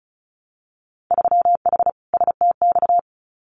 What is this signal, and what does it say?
Morse code, heard as a keyed tone, about 36 wpm, 700 Hz. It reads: 3H STX
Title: Winter Field Day Exchanges Encore 35wpm